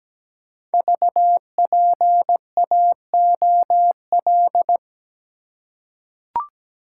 Morse code, heard as a keyed tone, about 17 wpm, 700 Hz. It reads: VPAOL
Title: Code Groups - 5 Letters - 17wpm